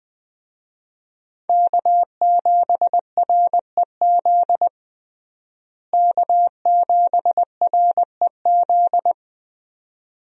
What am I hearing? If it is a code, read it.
Morse code, heard as a keyed tone, about 21 wpm, 700 Hz. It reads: K7REZ K7REZ
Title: USA Call Signs - 5 Characters - Repeated 2x - 20wpm